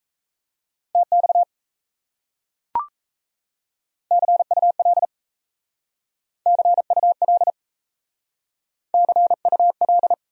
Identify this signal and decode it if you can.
Morse code, heard as a keyed tone, about 35 wpm, 700 Hz. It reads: TX CUL CUL CUL
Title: Speed-Racing Abbreviations - 28wpm